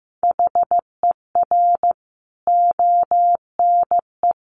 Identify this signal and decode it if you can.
Morse code, heard as a keyed tone, about 15 wpm, 700 Hz. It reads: HER ONE
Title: Sentences from Top 1500 Words 15wpm